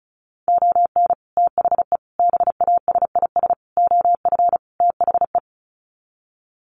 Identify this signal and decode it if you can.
Morse code, heard as a keyed tone, about 34 wpm, 700 Hz. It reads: ON THE BASIS OF THE
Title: Sets of 5 Words 35wpm